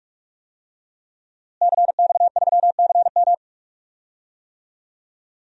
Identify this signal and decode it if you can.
Morse code, heard as a keyed tone, about 46 wpm, 700 Hz. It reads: CX3CM
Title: DXCC-100 Call Signs - 5 Characters - Repeated 3x - 45wpm